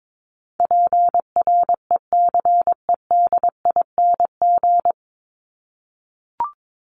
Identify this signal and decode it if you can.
Morse code, heard as a keyed tone, about 22 wpm, 700 Hz. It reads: PRECEDING
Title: ING Suffix - Top 100 Words with Base Words 22wpm